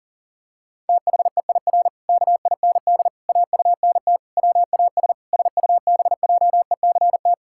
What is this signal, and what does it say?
Morse code, heard as a keyed tone, about 40 wpm, 700 Hz. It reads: THEIR KIND AUNT WAS SUBJECT
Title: Pangram Challenge v2 - 40wpm